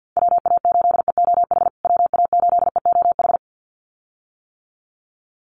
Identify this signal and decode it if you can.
Morse code, heard as a keyed tone, about 50 wpm, 700 Hz. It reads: WA8EOH WA8EOH
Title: USA Call Signs Repeated 2x 50wpm